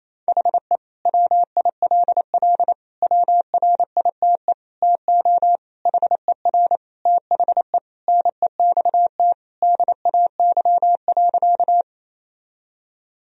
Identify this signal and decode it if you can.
Morse code, heard as a keyed tone, about 28 wpm, 700 Hz. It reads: HE WILL WRITE TO HER THE NEXT DAY.
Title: Sentences from Top 1000 Words - Review Part 1 - 28wpm